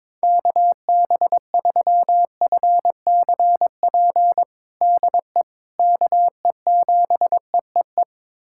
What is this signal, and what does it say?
Morse code, heard as a keyed tone, about 22 wpm, 700 Hz. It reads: KB3FCP DE KE7EEE
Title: USA Callsign DE Callsign 22wpm